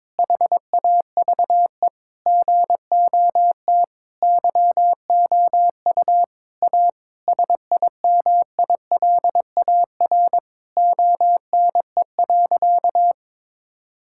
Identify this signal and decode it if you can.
Morse code, heard as a keyed tone, about 22 wpm, 700 Hz. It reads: HAVE GOT YOU A SIMILAR ONE.
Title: Sentences from Top 1000 Words - Review Part 5 - 22wpm